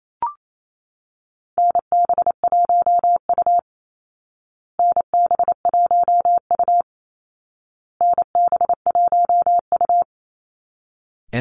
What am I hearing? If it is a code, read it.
Morse code, heard as a keyed tone, about 28 wpm, 700 Hz. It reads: NB1U NB1U NB1U
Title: Call Signs Repeated 2x - 28wpm